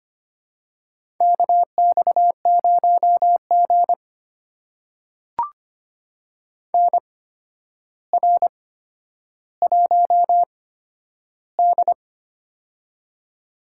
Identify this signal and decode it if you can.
Morse code, heard as a keyed tone, about 25 wpm, spaced as for 11 wpm, 700 Hz. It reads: KX0G NR1D
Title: USA Call Signs - 4 Characters - 8x Character Spacing - 25wpm